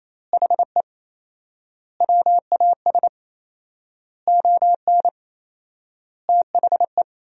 Morse code, heard as a keyed tone, about 28 wpm, 700 Hz. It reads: HE WAS ON THE
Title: 4x Word-Spacing - Sentences from Top 500 Words - 28wpm